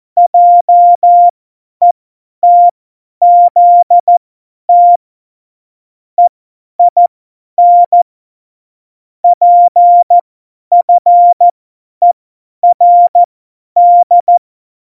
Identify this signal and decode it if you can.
Morse code, heard as a keyed tone, about 14 wpm, spaced as for 10 wpm, 700 Hz. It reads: JETZT EIN PFERD
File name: Morsekurs / Witze / Witz 05 Mai 2020, 50 F70